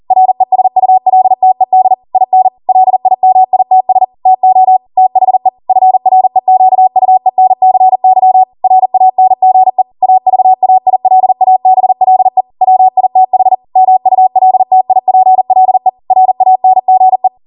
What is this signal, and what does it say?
Morse code, heard as a keyed tone, about 40 wpm, 750 Hz. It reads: RESULTED IN LIMITS TO THE FREQUENCY RANGE AVAILABLE WITH MULTIPLE RANGE